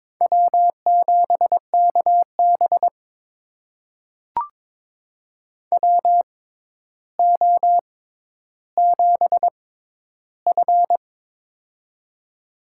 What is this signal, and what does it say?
Morse code, heard as a keyed tone, about 23 wpm, spaced as for 11 wpm, 700 Hz. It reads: W7KB WO7F
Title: USA Call Signs - 4 Characters - 6x Character Spacing - 22wpm